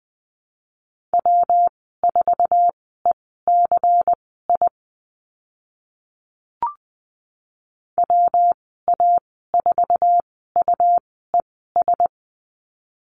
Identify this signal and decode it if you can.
Morse code, heard as a keyed tone, about 20 wpm, spaced as for 14 wpm, 700 Hz. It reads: W4ECI WA4UES
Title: 128 Lesson 21g - Number 6 - Call Signs 20wpm